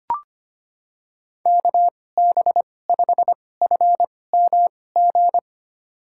Morse code, heard as a keyed tone, about 25 wpm, spaced as for 18 wpm, 700 Hz. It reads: KB5FMG
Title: USA Call Signs - 6 Characters - 2x Character Spacing - 25wpm